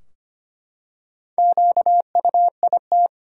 Morse code, heard as a keyed tone, about 25 wpm, 700 Hz. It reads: QUIT
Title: Four Letter Words - 25wpm